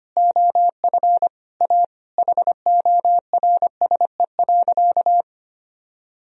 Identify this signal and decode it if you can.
Morse code, heard as a keyed tone, about 25 wpm, 700 Hz. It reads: OF A HORSE.
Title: Sentences from Top 1000 Words - Review Part 4 - 25wpm